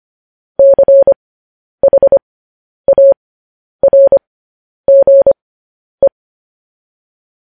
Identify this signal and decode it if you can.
Morse code, heard as a keyed tone, about 25 wpm, spaced as for 10 wpm, 550 Hz. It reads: CHARGE
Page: news-headlines-in-morse-code-at-10-wpm